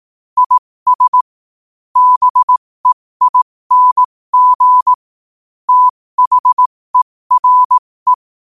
Morse code, heard as a keyed tone, about 18 wpm, spaced as for 15 wpm, 1000 Hz. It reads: IS BEING THERE